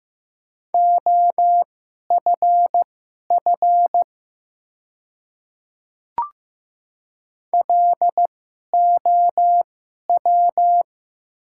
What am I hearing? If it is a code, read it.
Morse code, heard as a keyed tone, about 15 wpm, spaced as for 11 wpm, 700 Hz. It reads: OFF LOW
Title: 2x Character Spacing - Top 200 Words - 15wpm